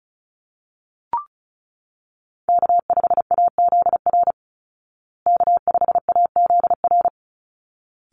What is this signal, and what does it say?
Morse code, heard as a keyed tone, about 34 wpm, 700 Hz. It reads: K5AZR K5AZR
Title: Call Signs Repeated 3x - 35wpm